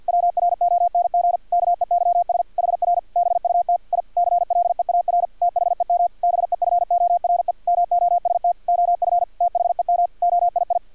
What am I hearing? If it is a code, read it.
Morse code, heard as a keyed tone, about 50 wpm, 700 Hz, with radio noise. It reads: WRONG KEYS HR BUT I CLEAR THEM BEFORE MOST OF THEM OII
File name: archived / CRSnet